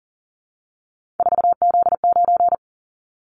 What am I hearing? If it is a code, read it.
Morse code, heard as a keyed tone, about 38 wpm, 700 Hz. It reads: 4Z9
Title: Callsign Prefixes for Top 50 DXCC - 40wpm